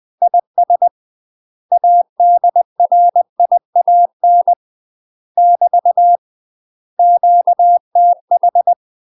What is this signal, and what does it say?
Morse code, heard as a keyed tone, about 20 wpm, 700 Hz. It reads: IS ADRIAN <BT> QTH
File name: UK MASTER QSOs TRAINING/20WPM